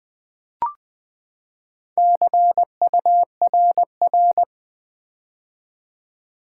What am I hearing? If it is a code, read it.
Morse code, heard as a keyed tone, about 20 wpm, 700 Hz. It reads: CURR
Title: Build A Word - 4 Letters At A Time - 10-Letter Words - 20wpm